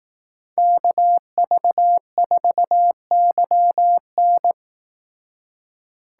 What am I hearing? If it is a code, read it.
Morse code, heard as a keyed tone, about 18 wpm, 700 Hz. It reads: KV4YN
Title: Speed Racing - USA 2x2 Call Signs 15wpm